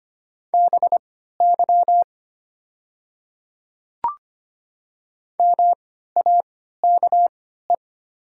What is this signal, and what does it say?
Morse code, heard as a keyed tone, about 25 wpm, spaced as for 14 wpm, 700 Hz. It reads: BY MAKE
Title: 3x Character Spacing - Top 100 Words - 25wpm